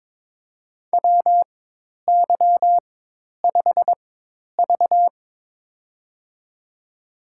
Morse code, heard as a keyed tone, about 22 wpm, spaced as for 10 wpm, 700 Hz. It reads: WY5V
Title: USA Call Signs - 4 Characters - 4x Character Spacing - 22wpm